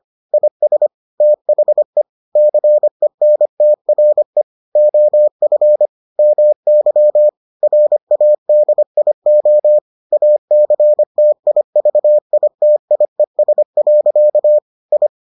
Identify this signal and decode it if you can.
Morse code, heard as a keyed tone, about 25 wpm, 600 Hz. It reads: IS THE CENTRE OF MY RADIO ACTIVITIES. I